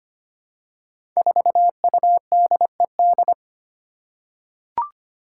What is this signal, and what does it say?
Morse code, heard as a keyed tone, about 25 wpm, 700 Hz. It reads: 4UDED
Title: Five Challenging Letter-Number Combinations 25wpm